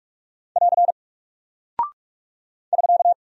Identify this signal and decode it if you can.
Morse code, heard as a keyed tone, about 44 wpm, 700 Hz. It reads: <AR> <SK>
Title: Single Character (Letter:Number:Punctuation:Prosign) Alt 45wpm